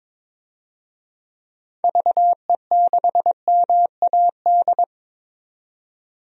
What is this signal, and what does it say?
Morse code, heard as a keyed tone, about 22 wpm, 700 Hz. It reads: VE6MAD
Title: DXCC-100 Call Signs - 6 Characters - 22wpm